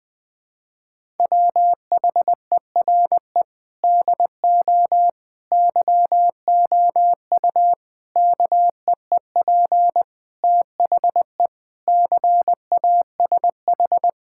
Morse code, heard as a keyed tone, about 20 wpm, 700 Hz. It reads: WHERE DO YOU KEEP THE CASH
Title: Sentences from Top 1000 Words - Review Part 5 - 20wpm